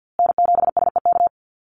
Morse code, heard as a keyed tone, about 50 wpm, 700 Hz. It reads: N7SEK
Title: USA Call Signs Repeated 2x 50wpm